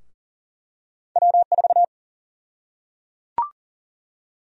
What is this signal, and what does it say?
Morse code, heard as a keyed tone, about 37 wpm, 700 Hz. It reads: W4
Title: USA Call Sign Prefixes - Repeated 2x - 40wpm